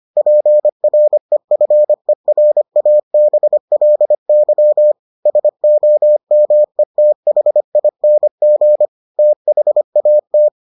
Morse code, heard as a keyed tone, about 25 wpm, 600 Hz. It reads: PREFERABLY SOMETHING THAT